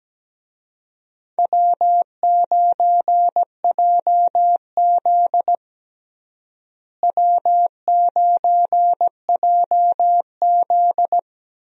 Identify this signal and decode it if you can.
Morse code, heard as a keyed tone, about 17 wpm, 700 Hz. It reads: W9JZ W9JZ
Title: USA Call Signs - 4 Characters - Repeated 2x - 17wpm